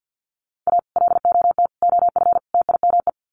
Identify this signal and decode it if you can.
Morse code, heard as a keyed tone, about 50 wpm, 700 Hz. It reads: A LOT OF TIME
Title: Sets of 4 Words 50wpm